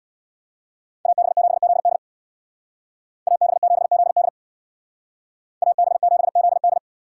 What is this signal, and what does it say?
Morse code, heard as a keyed tone, about 64 wpm, 700 Hz. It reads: AB7ZD AB7ZD AB7ZD